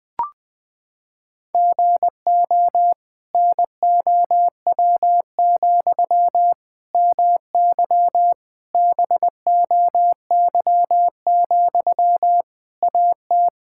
Morse code, heard as a keyed tone, about 21 wpm, 700 Hz. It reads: GO NOW, MY BOY, AT
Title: Sentences from Top 1000 Words - Review Part 5 - 20wpm